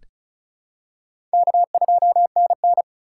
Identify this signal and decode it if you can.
Morse code, heard as a keyed tone, about 34 wpm, 700 Hz. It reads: K2NN
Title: USA Call Signs - 4 Characters - 4x Character Spacing - 35wpm